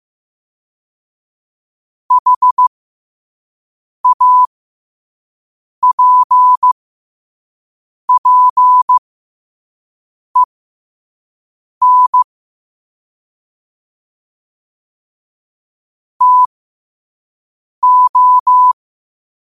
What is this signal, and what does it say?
Morse code, heard as a keyed tone, about 15 wpm, spaced as for 5 wpm, 1000 Hz. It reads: HAPPEN TO
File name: qotdfiles / 5wpm